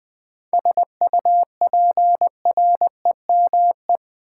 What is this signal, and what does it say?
Morse code, heard as a keyed tone, about 20 wpm, 690 Hz. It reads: SUPREME
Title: 7 Letter Words - First 3 Letters Given - 20wpm